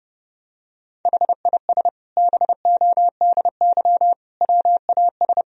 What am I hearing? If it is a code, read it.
Morse code, heard as a keyed tone, about 30 wpm, 700 Hz. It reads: HIS BODY WAS
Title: Sentences from Top 1000 Words - Review Part 2 - 30wpm